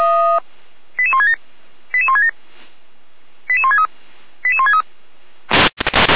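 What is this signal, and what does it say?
Five-tone selective call (ZVEI), heard as quick runs of five tones, tones of about 70 ms. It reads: 88167 88167 88163 88163
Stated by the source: ZVEI five-tone selective call received by SDR, address 88169, from feuerwehr-orlamuende.org fire brigade server.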